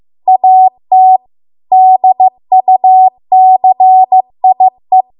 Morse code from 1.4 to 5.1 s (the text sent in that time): DUCIE